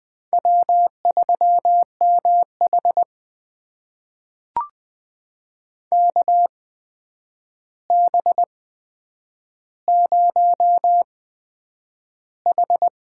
Morse code, heard as a keyed tone, about 20 wpm, spaced as for 9 wpm, 700 Hz. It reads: W3MH KB0H
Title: USA Call Signs - 4 Characters - 8x Character Spacing - 20wpm